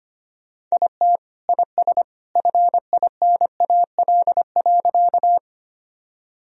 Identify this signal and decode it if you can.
Morse code, heard as a keyed tone, about 25 wpm, 700 Hz. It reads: IT IS FINAL.
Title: Sentences from Top 1000 Words - Review Part 2 - 25wpm